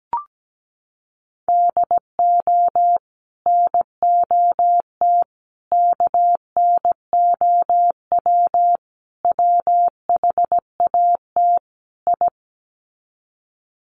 Morse code, heard as a keyed tone, about 17 wpm, 700 Hz. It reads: DO NOT KNOW WHAT I
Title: Sets of 5 Words 17wpm